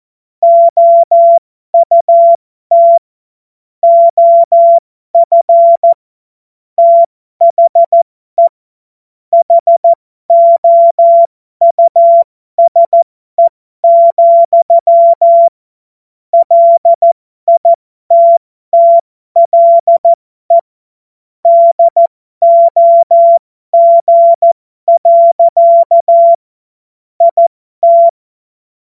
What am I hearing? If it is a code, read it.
Morse code, heard as a keyed tone, about 14 wpm, spaced as for 12 wpm, 680 Hz. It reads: OUT OF THE HOUSE, LITTLE DOG. IT
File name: Morsekurs / Witze